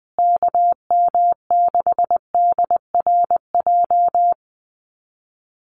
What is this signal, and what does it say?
Morse code, heard as a keyed tone, about 20 wpm, 700 Hz. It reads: KM6DRJ
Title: USA Call Signs Repeated 3x 20wpm